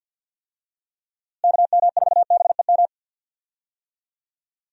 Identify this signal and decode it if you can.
Morse code, heard as a keyed tone, about 49 wpm, 700 Hz. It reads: KM4BEM
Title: USA Call Signs - 6 Characters - Repeated 2x - 50wpm